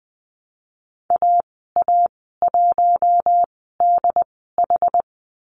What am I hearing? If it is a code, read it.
Morse code, heard as a keyed tone, about 20 wpm, spaced as for 14 wpm, 700 Hz. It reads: AA1DH
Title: 161 Lesson 26g - Letter g - Call Signs 20wpm